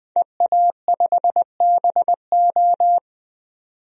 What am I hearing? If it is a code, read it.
Morse code, heard as a keyed tone, about 20 wpm, 700 Hz. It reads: EA5BO
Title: DXCC-100 Call Signs - 5 Characters - 20wpm